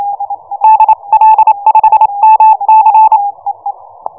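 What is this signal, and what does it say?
Morse code, heard as a keyed tone, about 28 wpm, 870 Hz. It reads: DL5MC
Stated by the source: QRP Morse code signal.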